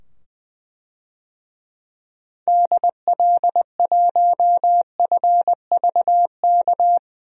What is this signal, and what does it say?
Morse code, heard as a keyed tone, about 20 wpm, 700 Hz. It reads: DL1FVK
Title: DXCC-100 Call Signs - 6 Characters - 20wpm